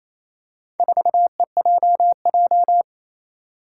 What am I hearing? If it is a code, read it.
Morse code, heard as a keyed tone, about 28 wpm, 700 Hz. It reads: 4EJJ
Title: Four Challenging Letter-Number Combinations 28wpm